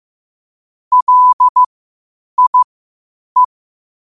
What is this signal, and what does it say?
Morse code, heard as a keyed tone, about 15 wpm, spaced as for 8 wpm, 1000 Hz. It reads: LIE
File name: qotdfiles / 8wpm